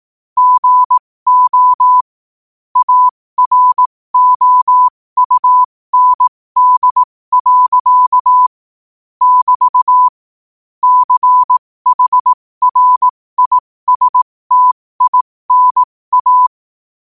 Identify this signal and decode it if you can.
Morse code, heard as a keyed tone, about 18 wpm, spaced as for 15 wpm, 1000 Hz. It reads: GO AROUND. <BT> CHRISTINA
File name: qotdfiles / 15wpm